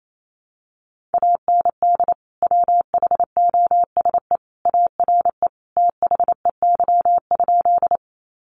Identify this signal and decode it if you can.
Morse code, heard as a keyed tone, about 28 wpm, 700 Hz. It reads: AND WHOSE ARE THEY?
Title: Sentences from Top 1200 Words 28wpm